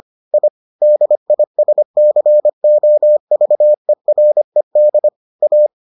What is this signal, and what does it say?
Morse code, heard as a keyed tone, about 25 wpm, 600 Hz. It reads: I DISCOVERED A